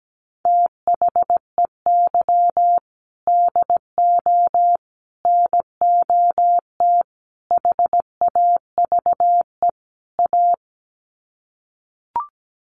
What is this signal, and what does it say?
Morse code, heard as a keyed tone, about 17 wpm, 700 Hz. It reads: THEY DO NOT HAVE A
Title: Sets of 5 Words 17wpm